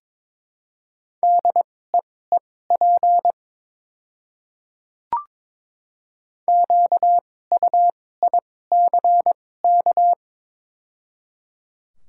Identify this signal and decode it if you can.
Morse code, heard as a keyed tone, about 22 wpm, spaced as for 16 wpm, 690 Hz. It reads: DEEP QUICK
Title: 2x Character Spacing - Top 500 Words - 22wpm